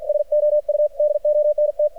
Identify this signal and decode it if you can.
Morse code, heard as a keyed tone, about 53 wpm, 600 Hz. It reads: SO U DONT
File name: archived / CRSnet